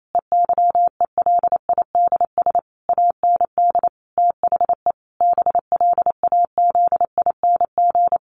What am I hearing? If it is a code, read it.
Morse code, heard as a keyed tone, about 28 wpm, 700 Hz. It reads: EYELIDS AND THE BLAZING
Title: A Sound Of Thunder 28wpm